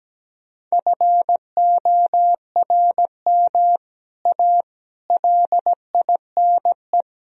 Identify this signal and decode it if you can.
Morse code, heard as a keyed tone, about 17 wpm, 700 Hz. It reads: FORM A LINE
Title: Sentences from Top 1000 Words - Review Part 1 - 17wpm